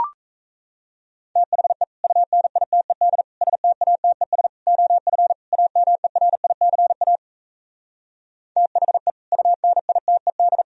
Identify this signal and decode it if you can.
Morse code, heard as a keyed tone, about 38 wpm, 700 Hz. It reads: THE UNITED STATES OF AMERICA THE UNITED